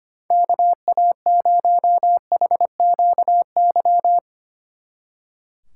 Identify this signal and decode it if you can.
Morse code, heard as a keyed tone, about 31 wpm, 700 Hz. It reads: KA0HQY
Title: Speed Racing - USA 2x3 Call Signs 25wpm